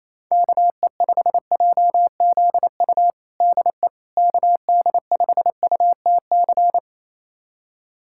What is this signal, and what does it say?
Morse code, heard as a keyed tone, about 28 wpm, 700 Hz. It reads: KE5JZU DE KD5UTC
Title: USA Callsign DE Callsign 28wpm